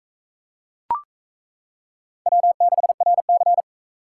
Morse code, heard as a keyed tone, about 40 wpm, 700 Hz. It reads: W6RC